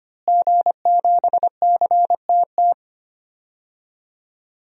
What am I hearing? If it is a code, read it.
Morse code, heard as a keyed tone, about 25 wpm, 700 Hz. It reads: G7CTT
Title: DXCC-100 Call Signs - 5 Characters - Repeated 3x - 25wpm